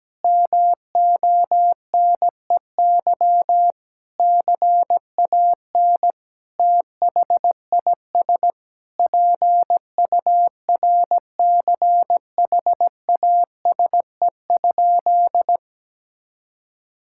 Morse code, heard as a keyed tone, about 17 wpm, 700 Hz. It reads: MONEY CAN THIS PURCHASE?
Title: Sentences from Top 1000 Words - Review Part 5 - 17wpm